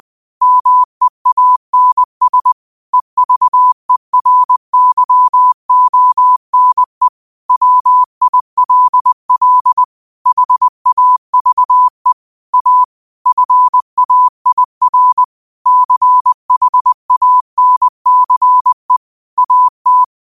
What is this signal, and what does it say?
Morse code, heard as a keyed tone, about 20 wpm, 1000 Hz. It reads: MEANS EVERYONE WILL HAVE A FAIR CHANCE AT
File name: qotdfiles / 20wpm